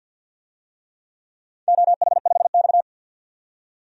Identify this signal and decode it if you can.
Morse code, heard as a keyed tone, about 53 wpm, 700 Hz. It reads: OH5X
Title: DXCC-100 Call Signs - 4 Characters - 50wpm